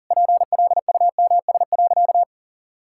Morse code, heard as a keyed tone, about 40 wpm, 700 Hz. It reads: PLUMS.
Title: Pangram Challenge v2 - 40wpm